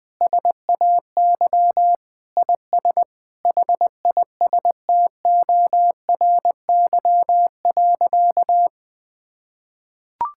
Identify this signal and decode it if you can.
Morse code, heard as a keyed tone, about 21 wpm, 700 Hz. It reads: SAY IS HISTORY.
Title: Sentences with 8-Word Phrases - 20wpm